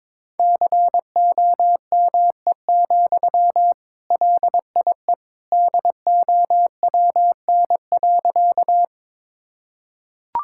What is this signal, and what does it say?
Morse code, heard as a keyed tone, about 22 wpm, 700 Hz. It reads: COME, LIE DOWN.
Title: Sentences from Top 1600 Words 22wpm